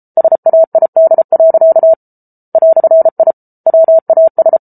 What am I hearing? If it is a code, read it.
Morse code, heard as a keyed tone, about 34 wpm, 650 Hz. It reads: SAID. "I WAS